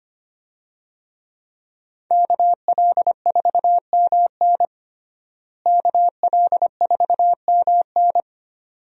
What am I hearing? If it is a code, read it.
Morse code, heard as a keyed tone, about 25 wpm, 700 Hz. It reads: KL4MN KL4MN
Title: USA Call Signs - 5 Characters - Repeated 2x - 25wpm